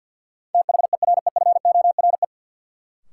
Morse code, heard as a keyed tone, about 51 wpm, 700 Hz. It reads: THEREFORE
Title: Sentences from Top 800 Words - New Word Drill - 50wpm